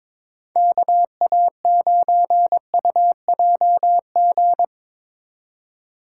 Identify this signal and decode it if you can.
Morse code, heard as a keyed tone, about 22 wpm, 700 Hz. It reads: KA9UJG
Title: USA Call Signs - 6 Characters - 3x Character Spacing - 22wpm